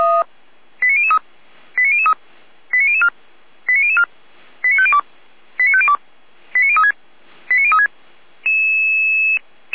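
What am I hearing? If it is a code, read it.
Five-tone selective call (ZVEI), heard as quick runs of five tones, tones of about 70 ms. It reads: 89003 89003 89004 89004 89502 89502 89036 89036
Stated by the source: ZVEI five-tone call received by SDR, archived by Feuerwehr Orlamünde (fire brigade).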